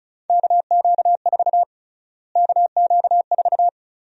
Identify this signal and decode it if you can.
Morse code, heard as a keyed tone, about 34 wpm, 700 Hz. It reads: KQ4 KQ4
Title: USA Call Sign Prefixes - Repeated 2x - 35wpm